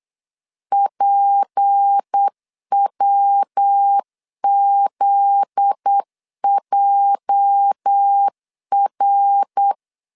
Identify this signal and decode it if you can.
Morse code, heard as a keyed tone, about 8 wpm, 800 Hz. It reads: PWZJR
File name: LEZIONI / 8 YJ45 / 8 YJ 8wpm 3313